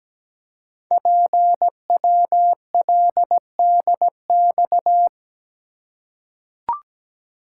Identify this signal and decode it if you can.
Morse code, heard as a keyed tone, about 17 wpm, 700 Hz. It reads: PWLDX
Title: Code Groups - 5 Letter:Numbers - 17wpm